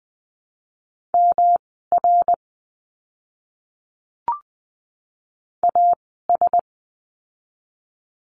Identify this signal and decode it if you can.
Morse code, heard as a keyed tone, about 20 wpm, spaced as for 15 wpm, 700 Hz. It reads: MR AS
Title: 244 Lesson 39f - Letter z - Words 20wpm